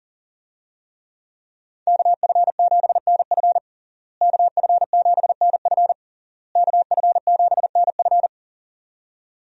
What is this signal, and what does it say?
Morse code, heard as a keyed tone, about 37 wpm, 700 Hz. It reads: KF7NF KF7NF KF7NF
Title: USA Call Signs - 5 Characters - Repeated 3x - 40wpm